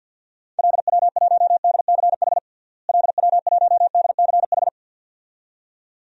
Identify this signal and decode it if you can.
Morse code, heard as a keyed tone, about 49 wpm, 700 Hz. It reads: LW1DCH LW1DCH
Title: DXCC-100 Call Signs - 6 Characters - Repeated 2x - 50wpm